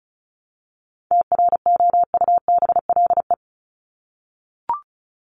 Morse code, heard as a keyed tone, about 34 wpm, 700 Hz. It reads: TROUBLE
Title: Top 1001-1250 Words 35wpm